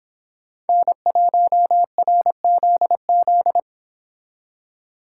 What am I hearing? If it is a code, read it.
Morse code, heard as a keyed tone, about 26 wpm, 700 Hz. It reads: N1RZZ
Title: Speed Racing - USA 1x3 Call Signs 22wpm